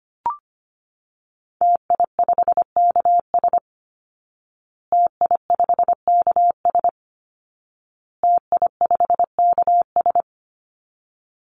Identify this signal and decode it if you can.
Morse code, heard as a keyed tone, about 25 wpm, 700 Hz. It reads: TI5KS TI5KS TI5KS
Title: Call Signs Repeated 2x - 25wpm